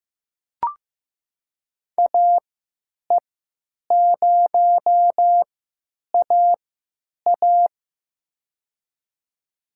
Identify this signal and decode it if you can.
Morse code, heard as a keyed tone, about 15 wpm, spaced as for 9 wpm, 700 Hz. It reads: AE0AA
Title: USA Call Signs - 5 Characters - 3x Character Spacing - 15wpm